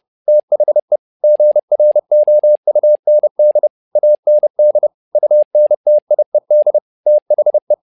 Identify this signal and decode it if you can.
Morse code, heard as a keyed tone, about 30 wpm, 600 Hz. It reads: THE GROUND AND UNTIED THE
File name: learning center / tower trouble / tower trouble 30wpm